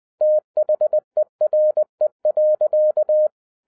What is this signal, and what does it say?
Morse code, heard as a keyed tone, about 20 wpm, 610 Hz. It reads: THERE.
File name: 2011-10-mp3 / n03-TARO JR0QWW 20wpm